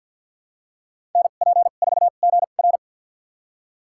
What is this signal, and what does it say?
Morse code, heard as a keyed tone, about 52 wpm, spaced as for 37 wpm, 700 Hz. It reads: NP4GR
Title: USA Call Signs - 2x Character Spacing - 50wpm